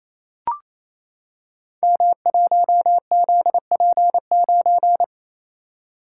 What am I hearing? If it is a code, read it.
Morse code, heard as a keyed tone, about 28 wpm, 700 Hz. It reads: M1ZP9
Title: Code Groups - 5 Letter:Numbers - 28wpm